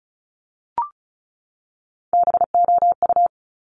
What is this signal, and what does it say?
Morse code, heard as a keyed tone, about 36 wpm, 700 Hz. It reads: BOU
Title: Top Three Letter Combinations in Words 35wpm